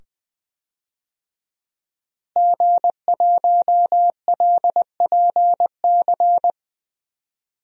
Morse code, heard as a keyed tone, about 20 wpm, 700 Hz. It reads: G1LPC
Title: DXCC-100 Call Signs - 5 Characters - Repeated 2x - 20wpm